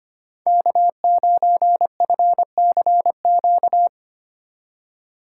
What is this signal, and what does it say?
Morse code, heard as a keyed tone, about 25 wpm, 700 Hz. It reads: K9FCQ
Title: USA Call Signs - 5 Characters - 4x Character Spacing - 25wpm